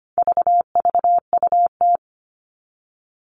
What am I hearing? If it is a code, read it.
Morse code, heard as a keyed tone, about 25 wpm, 700 Hz. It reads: VVUT
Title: All Cut Numbers - 4-Digit 25wpm